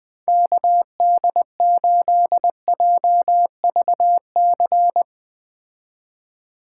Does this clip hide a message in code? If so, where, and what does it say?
Yes: Morse code from 0.3 to 5.0 s, heard as a keyed tone, about 21 wpm, 700 Hz. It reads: KD8JVC